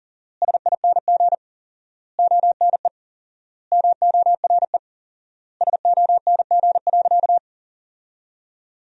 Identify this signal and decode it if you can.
Morse code, heard as a keyed tone, about 40 wpm, 700 Hz. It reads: SING ONE MORE SONG.
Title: 4x Word-Spacing - Sentences from Top 400 Words - 40wpm